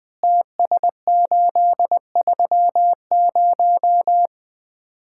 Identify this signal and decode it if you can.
Morse code, heard as a keyed tone, about 20 wpm, 700 Hz. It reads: TS830
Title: Common QSO Phrases - 2x Word-Spacing - 20wpm